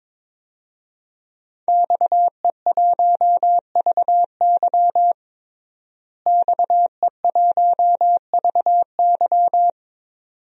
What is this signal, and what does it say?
Morse code, heard as a keyed tone, about 22 wpm, 700 Hz. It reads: XE1VY XE1VY
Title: DXCC-100 Call Signs - 5 Characters - Repeated 2x - 22wpm